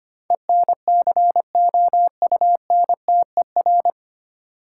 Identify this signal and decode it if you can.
Morse code, heard as a keyed tone, about 25 wpm, 700 Hz. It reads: ENCOUNTER
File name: practice / 25wpm / 9 Letter Words - First 3 Letters Given - 25wpm